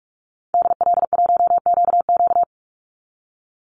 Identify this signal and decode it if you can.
Morse code, heard as a keyed tone, about 44 wpm, 700 Hz. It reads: DL1QQ
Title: Call Signs Repeated 3x - 45wpm